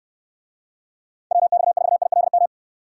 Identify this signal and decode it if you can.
Morse code, heard as a keyed tone, about 69 wpm, 700 Hz. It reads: WX4ILM